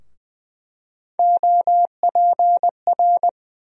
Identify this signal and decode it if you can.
Morse code, heard as a keyed tone, about 20 wpm, 700 Hz. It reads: OPR
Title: USA Call Sign Postfixes - 3 Letters 20wpm